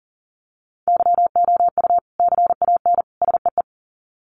Morse code, heard as a keyed tone, about 40 wpm, 700 Hz. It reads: YOU CAN SEE
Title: Sets of 3 Words 40wpm